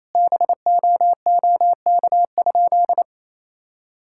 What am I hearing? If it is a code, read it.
Morse code, heard as a keyed tone, about 28 wpm, 700 Hz. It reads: BOOK?
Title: Sentences from Top 1000 Words - Review Part 5 - 28wpm